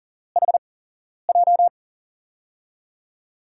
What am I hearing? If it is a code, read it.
Morse code, heard as a keyed tone, about 40 wpm, spaced as for 11 wpm, 700 Hz. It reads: HJ
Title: USA Call Signs - 6 Characters - 8x Character Spacing - 40wpm